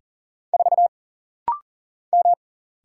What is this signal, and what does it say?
Morse code, heard as a keyed tone, about 40 wpm, 700 Hz. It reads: 4 M
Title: Single Letter-Number Alt - Rapid-Fire 40wpm